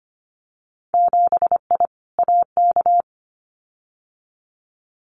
Morse code, heard as a keyed tone, about 25 wpm, 700 Hz. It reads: 7I AK
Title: Winter Field Day Exchanges Encore 25wpm